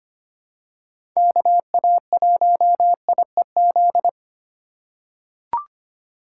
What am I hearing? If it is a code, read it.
Morse code, heard as a keyed tone, about 25 wpm, 700 Hz. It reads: KA1IEZ
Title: USA Call Signs - 2x Character Spacing - 25wpm